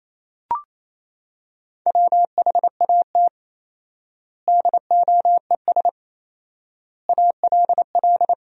Morse code, heard as a keyed tone, about 28 wpm, 700 Hz. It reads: WHAT DOES ALL
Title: 4x Word-Spacing - Sentences from Top 200 Words - 28wpm